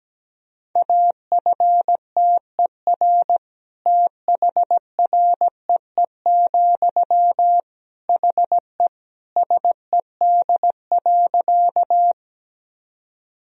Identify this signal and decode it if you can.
Morse code, heard as a keyed tone, about 17 wpm, 700 Hz. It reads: AFTER THREE, HE SED.
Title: Sentences from Top 200 Words and Abbreviations 17wpm